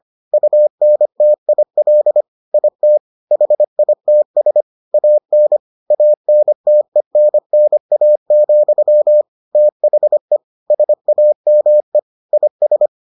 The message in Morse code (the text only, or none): UNTIL IT HITS AN ANTENNA, THE SAME IS